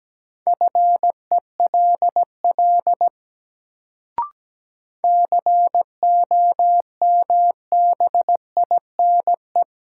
Morse code, heard as a keyed tone, about 17 wpm, 700 Hz. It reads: FELL COMBINE
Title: Sentences from Top 1600 Words - New Word Drill Alt - 17wpm